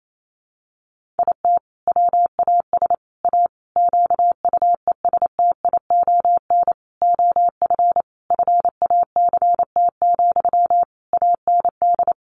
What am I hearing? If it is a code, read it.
Morse code, heard as a keyed tone, about 28 wpm, 700 Hz. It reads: IT WAS A QUESTION OF FACT, AND
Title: Sentences from Top 1400 Words 28wpm